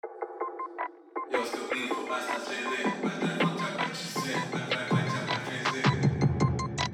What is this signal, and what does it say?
There is no code.